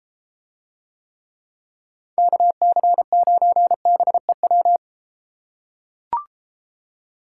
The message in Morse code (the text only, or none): KC9BEW